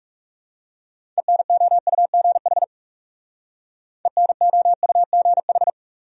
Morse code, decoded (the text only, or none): ENOUGH ENOUGH